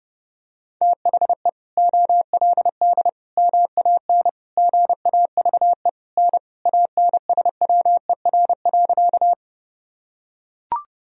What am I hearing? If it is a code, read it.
Morse code, heard as a keyed tone, about 32 wpm, 700 Hz. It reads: THE OLD MAN GAVE N ANSWER.
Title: Sentences from Top 500 Words and Abbreviations 30wpm